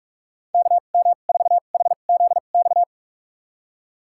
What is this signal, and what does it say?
Morse code, heard as a keyed tone, about 43 wpm, spaced as for 32 wpm, 700 Hz. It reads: KM4HZX
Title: USA Call Signs - 6 Characters - 2x Character Spacing - 45wpm